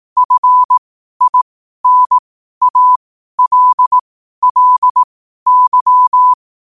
Morse code, heard as a keyed tone, about 18 wpm, spaced as for 12 wpm, 1000 Hz. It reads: FINALLY
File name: qotdfiles / 12wpm